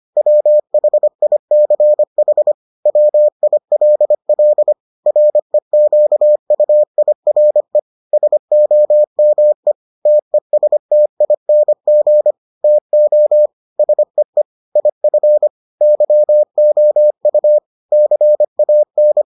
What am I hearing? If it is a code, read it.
Morse code, heard as a keyed tone, about 25 wpm, 600 Hz. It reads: WHICH WILL REQUIRE SOME TESTING TO SEE IF YOU CAN